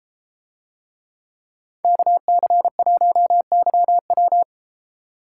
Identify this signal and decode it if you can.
Morse code, heard as a keyed tone, about 32 wpm, 700 Hz. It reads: KC1YW